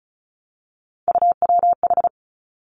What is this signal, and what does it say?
Morse code, heard as a keyed tone, about 36 wpm, 700 Hz. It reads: UWH
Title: Three Challenging Letter-Number Combinations 35wpm